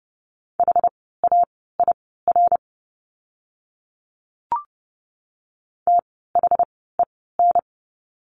Morse code, heard as a keyed tone, about 30 wpm, spaced as for 17 wpm, 700 Hz. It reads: HAIR THEN
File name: Lesson 17f - Letter u - Words 30wpm